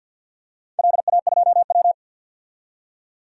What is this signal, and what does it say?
Morse code, heard as a keyed tone, about 52 wpm, 700 Hz. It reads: LA2W
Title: DXCC-100 Call Signs - 4 Characters - 50wpm